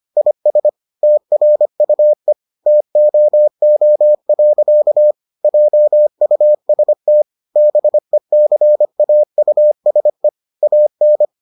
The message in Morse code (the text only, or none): IS TRUE TOO. JUST BECAUSE AN